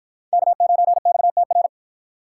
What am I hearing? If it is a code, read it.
Morse code, heard as a keyed tone, about 53 wpm, 700 Hz. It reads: K9XTR